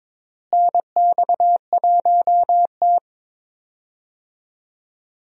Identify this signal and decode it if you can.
Morse code, heard as a keyed tone, about 22 wpm, 700 Hz. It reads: NX1T
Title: USA Call Signs - 4 Characters - Repeated 3x - 22wpm